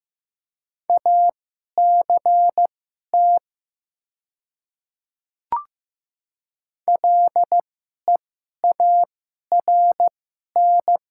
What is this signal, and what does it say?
Morse code, heard as a keyed tone, about 15 wpm, spaced as for 11 wpm, 700 Hz. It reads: ACT LEARN
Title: 2x Character Spacing - Top 500 Words - 15wpm